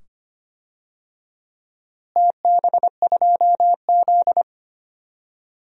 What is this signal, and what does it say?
Morse code, heard as a keyed tone, about 25 wpm, 700 Hz. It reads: TB2Z
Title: DXCC-100 Call Signs - 4 Characters - Repeated 2x - 25wpm